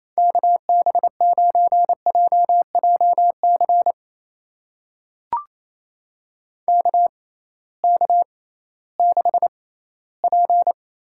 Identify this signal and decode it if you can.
Morse code, heard as a keyed tone, about 28 wpm, spaced as for 16 wpm, 700 Hz. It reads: KB9JJC KK6P
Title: USA Call Signs - 6 Characters - 6x Character Spacing - 28wpm